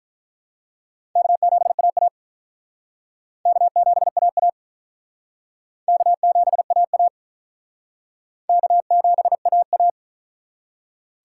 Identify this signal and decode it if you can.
Morse code, heard as a keyed tone, about 48 wpm, 700 Hz. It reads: K7AA K7AA K7AA K7AA